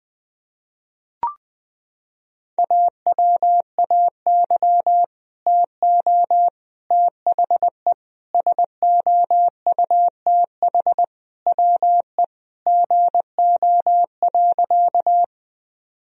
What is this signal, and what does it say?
Morse code, heard as a keyed tone, about 19 wpm, 700 Hz. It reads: AWAY TO THE SOUTH WE GO.
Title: Sentences from Top 1000 Words - Review Part 2 - 20wpm